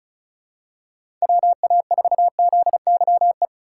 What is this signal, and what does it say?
Morse code, heard as a keyed tone, about 34 wpm, 700 Hz. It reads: WA4ZYE
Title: USA Call Signs - 2x Character Spacing - 35wpm